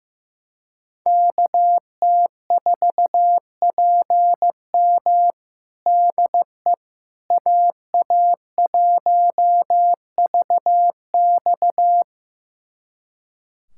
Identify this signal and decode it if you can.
Morse code, heard as a keyed tone, about 15 wpm, 700 Hz. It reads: KT4PM DE AA1VX
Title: USA Callsign DE Callsign - 5 Characters - 15wpm